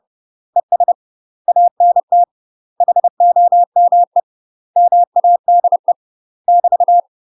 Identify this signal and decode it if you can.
Morse code, heard as a keyed tone, about 30 wpm, 700 Hz. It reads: ES ANT HOME MADE <BT>
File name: UK MASTER QSOs TRAINING / 30WPM / UK MASTER QSOs 0041